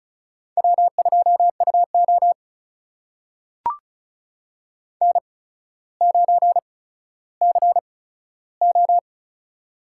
Morse code, heard as a keyed tone, about 36 wpm, spaced as for 15 wpm, 700 Hz. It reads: W2UO N9CO
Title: USA Call Signs - 4 Characters - 8x Character Spacing - 35wpm